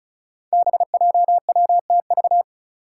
Morse code, heard as a keyed tone, about 34 wpm, 700 Hz. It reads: BJWTV
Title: Five Challenging Letter-Number Combinations 35wpm